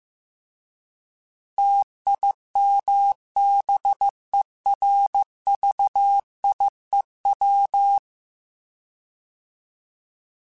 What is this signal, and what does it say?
Morse code, heard as a keyed tone, about 15 wpm, 800 Hz. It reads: TIMBERVIEW